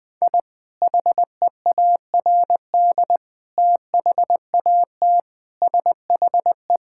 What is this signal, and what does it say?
Morse code, heard as a keyed tone, about 20 wpm, 700 Hz. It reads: I HEARD THAT SHE
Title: Sentences with 3-Word Phrases - 20wpm